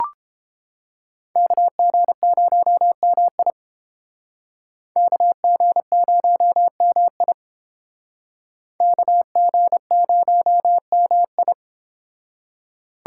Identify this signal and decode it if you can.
Morse code, heard as a keyed tone, about 32 wpm, 700 Hz. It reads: KG0MI KG0MI KG0MI